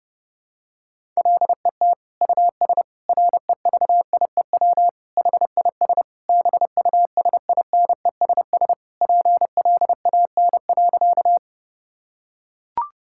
Morse code, heard as a keyed tone, about 32 wpm, 700 Hz. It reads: LET US REVIEW HIS BUSINESS PLAN.
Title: Sentences from Top 1000 Words - Review Part 6 - 30wpm